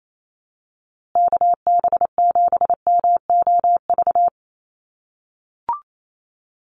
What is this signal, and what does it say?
Morse code, heard as a keyed tone, about 28 wpm, 700 Hz. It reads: KB7MOV
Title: USA Call Signs Repeated 3x 28wpm